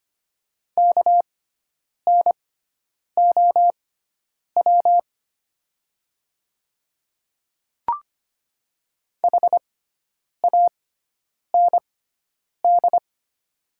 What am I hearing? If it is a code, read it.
Morse code, heard as a keyed tone, about 25 wpm, spaced as for 9 wpm, 700 Hz. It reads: KNOW HAND
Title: 6x Character Spacing - Top 300 Words - 25wpm